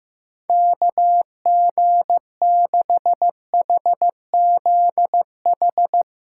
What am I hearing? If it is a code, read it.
Morse code, heard as a keyed tone, about 16 wpm, 700 Hz. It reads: KG6HZH
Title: USA Call Signs - 6 Characters - 3x Character Spacing - 15wpm